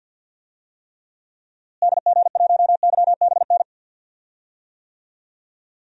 Morse code, heard as a keyed tone, about 47 wpm, 700 Hz. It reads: DG1YBN
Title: DXCC-100 Call Signs - 6 Characters - Repeated 2x - 50wpm